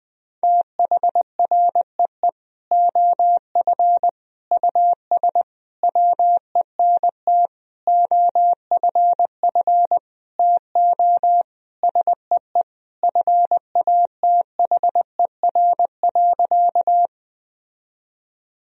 Morse code, heard as a keyed tone, about 20 wpm, 700 Hz. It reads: THREE OF US WENT OFF TO SEE FATHER.
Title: Sentences from Top 1000 Words - Review Part 1 - 20wpm